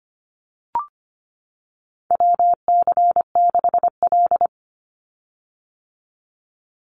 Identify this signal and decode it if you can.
Morse code, heard as a keyed tone, about 25 wpm, 700 Hz. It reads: WC6L
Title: USA Call Signs 25wpm